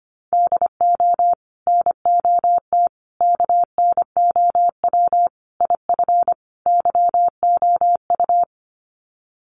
Morse code, heard as a keyed tone, about 25 wpm, 700 Hz. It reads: DO NOT KNOW IF YOU
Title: Sets of 5 Words 25wpm